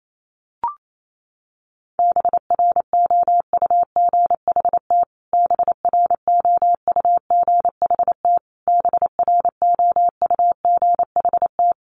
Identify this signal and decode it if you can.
Morse code, heard as a keyed tone, about 28 wpm, 700 Hz. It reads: BROUGHT BROUGHT BROUGHT
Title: Top 500 Words Repeated 28wpm